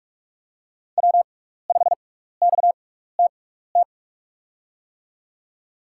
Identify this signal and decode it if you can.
Morse code, heard as a keyed tone, about 44 wpm, spaced as for 15 wpm, 700 Hz. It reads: W5XTT
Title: USA Call Signs - 5 Characters - 6x Character Spacing - 45wpm